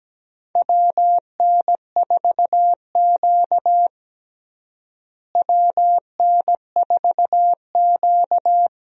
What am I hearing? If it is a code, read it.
Morse code, heard as a keyed tone, about 17 wpm, 700 Hz. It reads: WN4Q WN4Q
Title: USA Call Signs - 4 Characters - Repeated 3x - 17wpm